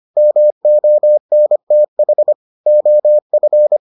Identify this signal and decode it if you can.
Morse code, heard as a keyed tone, about 25 wpm, 600 Hz. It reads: MONTH OF